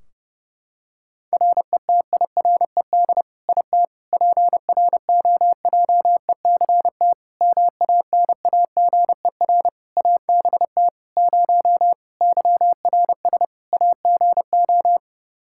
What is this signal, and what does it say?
Morse code, heard as a keyed tone, about 30 wpm, 700 Hz. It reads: RETIRED IT PROJECT MANAGER ABT 0 YRS AGO
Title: Practice QSO - Exchange 3 - More Info - 30wpm